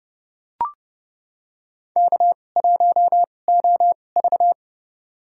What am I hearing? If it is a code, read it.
Morse code, heard as a keyed tone, about 29 wpm, spaced as for 21 wpm, 700 Hz. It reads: K1OV
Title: USA Call Signs - 4 Characters - 2x Character Spacing - 30wpm